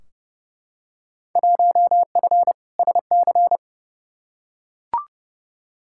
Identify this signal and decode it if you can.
Morse code, heard as a keyed tone, about 32 wpm, 700 Hz. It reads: 1F SC
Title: Field Day Exchanges Encore - 30wpm